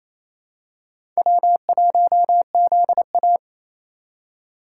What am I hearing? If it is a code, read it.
Morse code, heard as a keyed tone, about 28 wpm, 700 Hz. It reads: W1ZA
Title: USA Call Signs - 3x Character Spacing - 28wpm